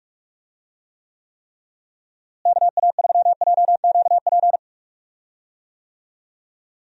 Morse code, heard as a keyed tone, about 46 wpm, 700 Hz. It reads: KA3JQP
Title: USA Call Signs - 6 Characters - Repeated 3x - 45wpm